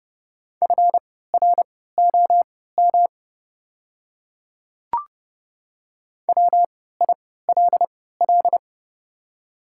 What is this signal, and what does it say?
Morse code, heard as a keyed tone, about 30 wpm, spaced as for 17 wpm, 690 Hz. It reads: FROM WILL
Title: 3x Character Spacing - Top 100 Words - 30wpm